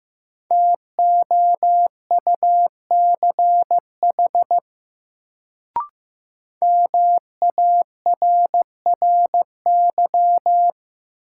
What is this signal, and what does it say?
Morse code, heard as a keyed tone, about 15 wpm, 700 Hz. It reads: TOUCH MARRY
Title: Sentences from Top 1200 Words - New Word Drill Alt - 15wpm